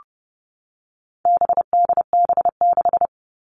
Morse code, heard as a keyed tone, about 30 wpm, 700 Hz. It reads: BDB6
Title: All Cut Numbers - 4-Digit 30wpm